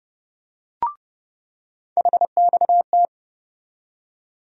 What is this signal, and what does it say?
Morse code, heard as a keyed tone, about 30 wpm, 700 Hz. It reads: HXT
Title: USA Call Sign Postfixes - 3 Letters 30wpm